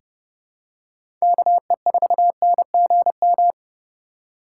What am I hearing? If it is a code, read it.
Morse code, heard as a keyed tone, about 30 wpm, 700 Hz. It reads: KE4NGM
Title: USA Call Signs - 6 Characters - 4x Character Spacing - 30wpm